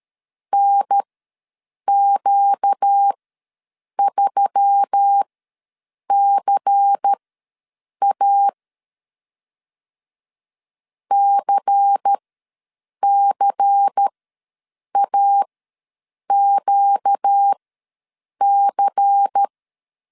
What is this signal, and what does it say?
Morse code, heard as a keyed tone, about 13 wpm, spaced as for 7 wpm, 800 Hz. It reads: NQ3CA CCAQC
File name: LEZIONI / 5 LBVC / 5 LBVC ws 12wpm 3159